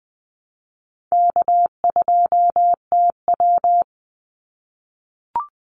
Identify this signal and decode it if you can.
Morse code, heard as a keyed tone, about 20 wpm, 700 Hz. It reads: K2TW
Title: Call Signs - 20wpm